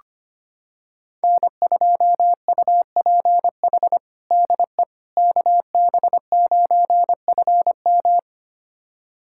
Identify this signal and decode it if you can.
Morse code, heard as a keyed tone, about 25 wpm, 700 Hz. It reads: N2UPH DE KB9FM
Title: USA Callsign DE Callsign - 5 Characters - 25wpm